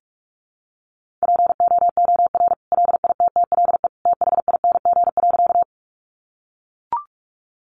Morse code, heard as a keyed tone, about 44 wpm, 700 Hz. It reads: POOR LITTLE THING.
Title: Sentences from Top 1000 Words 45wpm